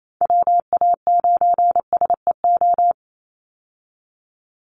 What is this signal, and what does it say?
Morse code, heard as a keyed tone, about 28 wpm, 700 Hz. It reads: WA9SEO
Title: Call Signs Repeated 2x - 28wpm